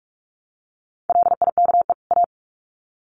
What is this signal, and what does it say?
Morse code, heard as a keyed tone, about 44 wpm, 700 Hz. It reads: LIKE A
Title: Sets of 2 Words 45wpm